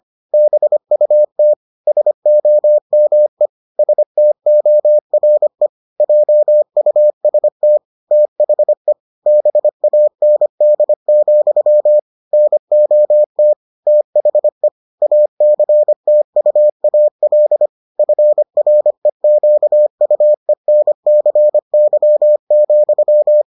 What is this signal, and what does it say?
Morse code, heard as a keyed tone, about 25 wpm, 600 Hz. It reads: BUT SOME STORE JUST THE BAND, NOT THE ACTUAL FREQUENCY,